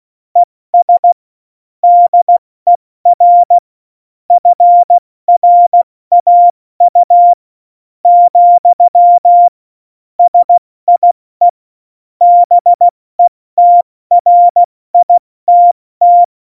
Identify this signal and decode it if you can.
Morse code, heard as a keyed tone, about 16 wpm, 700 Hz. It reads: ES DER FRAU , SIE BETRITT
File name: Morsekurs / Witze / Witz 23. Oktober 2018, 70 F80